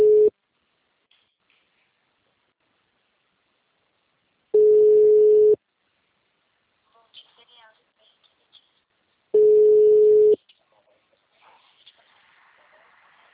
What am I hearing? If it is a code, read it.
Ringback tone (Europe, 425 Hz), with speech.